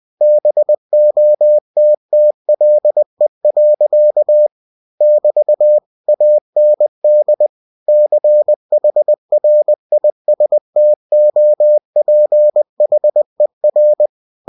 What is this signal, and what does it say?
Morse code, heard as a keyed tone, about 20 wpm, 600 Hz. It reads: BOTTLE. <BT> AND CHRISTOPHER
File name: cw-books / winnie-the-pooh / winnie-20wpm-page-0020